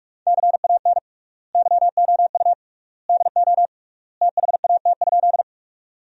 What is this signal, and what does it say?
Morse code, heard as a keyed tone, about 42 wpm, 700 Hz. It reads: CAN YOU DO THAT?
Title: 3x Word-Spacing - Sentences from Top 100 Words - 45wpm